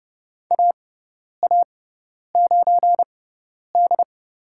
Morse code, heard as a keyed tone, about 31 wpm, spaced as for 10 wpm, 700 Hz. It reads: AA9D
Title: USA Call Signs - 4 Characters - 6x Character Spacing - 30wpm